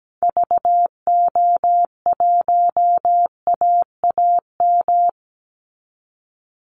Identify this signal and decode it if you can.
Morse code, heard as a keyed tone, about 17 wpm, 700 Hz. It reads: VO1AAM
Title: Call Signs Repeated 3x - 17wpm